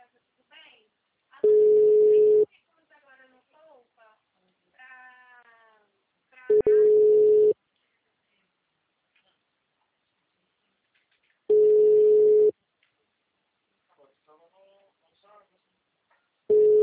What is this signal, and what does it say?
Ringback tone (Europe, 425 Hz), with speech.